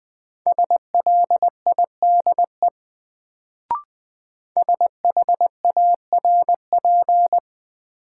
Morse code, heard as a keyed tone, about 20 wpm, 700 Hz. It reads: SLIDE SHARP
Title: Sentences from Top 1600 Words - New Word Drill - 20wpm